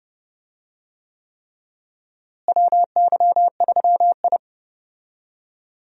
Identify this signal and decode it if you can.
Morse code, heard as a keyed tone, about 30 wpm, 700 Hz. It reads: WY3I
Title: USA Call Signs - 4 Characters - Repeated 3x - 30wpm